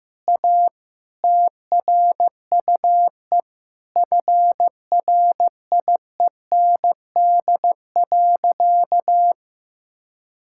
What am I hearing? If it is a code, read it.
Morse code, heard as a keyed tone, about 15 wpm, 700 Hz. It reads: A TRUE FRIEND.
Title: Sentences from Top 1000 Words - Review Part 6 - 15wpm